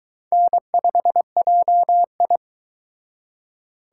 Morse code, heard as a keyed tone, about 22 wpm, 700 Hz. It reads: N5JI